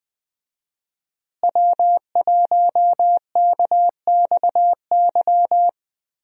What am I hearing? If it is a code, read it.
Morse code, heard as a keyed tone, about 20 wpm, 700 Hz. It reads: W1KXY
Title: USA Call Signs - 5 Characters - Repeated 3x - 20wpm